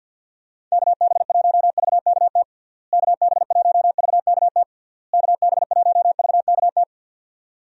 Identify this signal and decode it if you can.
Morse code, heard as a keyed tone, about 50 wpm, 700 Hz. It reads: KB1VKT KB1VKT KB1VKT
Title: USA Call Signs - 6 Characters - Repeated 3x - 50wpm